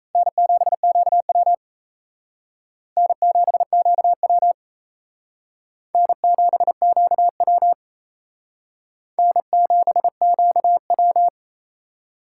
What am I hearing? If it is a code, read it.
Morse code, heard as a keyed tone, about 37 wpm, 700 Hz. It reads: N7QW N7QW N7QW N7QW